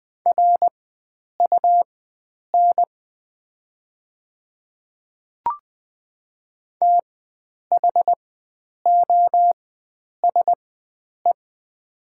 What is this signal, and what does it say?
Morse code, heard as a keyed tone, about 20 wpm, spaced as for 9 wpm, 700 Hz. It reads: RUN THOSE
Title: 4x Character Spacing - Top 500 Words - 20wpm